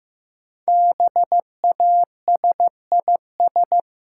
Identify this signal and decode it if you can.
Morse code, heard as a keyed tone, about 15 wpm, 700 Hz. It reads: BASIS
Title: Sentences from Top 1000 Words - Review Part 5 - New Word Drill - 15wpm